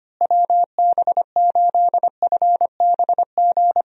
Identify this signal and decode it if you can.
Morse code, heard as a keyed tone, about 25 wpm, 700 Hz. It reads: WB8FBG
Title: USA Call Signs - 6 Characters - Repeated 2x - 25wpm